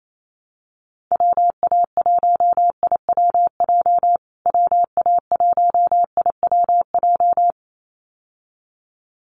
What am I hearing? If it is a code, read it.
Morse code, heard as a keyed tone, about 28 wpm, 700 Hz. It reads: WA1IWJ WA1IWJ
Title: USA Call Signs Repeated 2x 28wpm